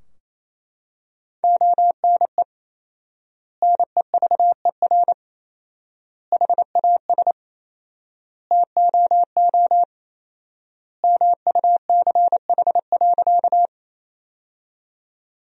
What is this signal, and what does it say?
Morse code, heard as a keyed tone, about 28 wpm, 700 Hz. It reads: ONE NEVER HAS TOO MUCH.
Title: 4x Word-Spacing - Sentences from Top 300 Words - 28wpm